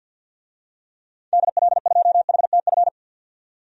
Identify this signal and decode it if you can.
Morse code, heard as a keyed tone, about 47 wpm, 700 Hz. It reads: DL2HTF
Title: DXCC-100 Call Signs - 6 Characters - 50wpm